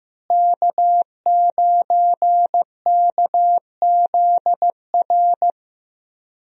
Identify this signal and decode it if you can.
Morse code, heard as a keyed tone, about 15 wpm, 700 Hz. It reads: K9KZR
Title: USA Call Signs - 5 Characters - Repeated 2x - 15wpm